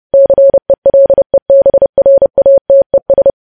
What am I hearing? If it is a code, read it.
Morse code, heard as a keyed tone, about 30 wpm, 550 Hz. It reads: CELEBRATES
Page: news headlines in morse code at 30 wpm